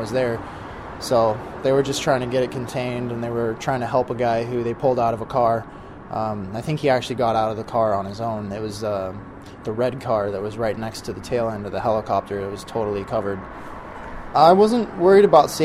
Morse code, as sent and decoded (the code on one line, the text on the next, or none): none